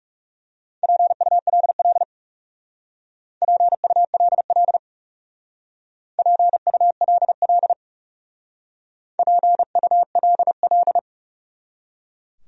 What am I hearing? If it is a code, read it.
Morse code, heard as a keyed tone, about 37 wpm, 700 Hz. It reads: PULL PULL PULL PULL
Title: Speed-Racing Top 500 Words 30wpm